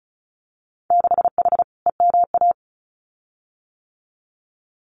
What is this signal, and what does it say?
Morse code, heard as a keyed tone, about 34 wpm, 700 Hz. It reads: 6H EMA
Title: Winter Field Day Exchanges Encore 35wpm